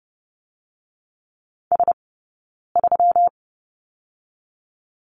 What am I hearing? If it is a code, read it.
Morse code, heard as a keyed tone, about 30 wpm, spaced as for 9 wpm, 700 Hz. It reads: S3
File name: Lesson 20d - Number 3 - Two Characters 30wpm